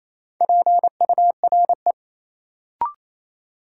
Morse code, heard as a keyed tone, about 28 wpm, 700 Hz. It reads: PURE
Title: Sentences from Top 1500 Words - New Word Drill Alt - 28wpm